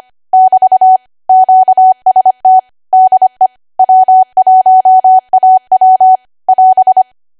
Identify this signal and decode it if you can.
Morse code, heard as a keyed tone, about 25 wpm, 750 Hz. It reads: <BT> QST DE W1AW <AS>
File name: cwa / W1AW-Level3 / 25WPM